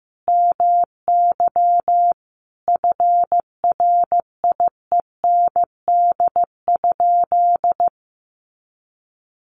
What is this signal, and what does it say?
Morse code, heard as a keyed tone, about 15 wpm, 700 Hz. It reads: MY FRIEND?
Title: Sentences from Top 300 Words 15wpm